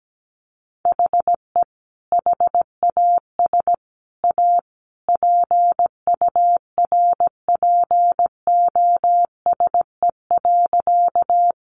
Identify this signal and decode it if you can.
Morse code, heard as a keyed tone, about 17 wpm, 700 Hz. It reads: HE HAS A PURPOSE.
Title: Sentences from Top 800 Words 17wpm